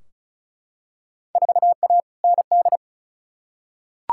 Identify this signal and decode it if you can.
Morse code, heard as a keyed tone, about 36 wpm, 700 Hz. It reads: 4A ND
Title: Field Day Exchanges Encore - 35wpm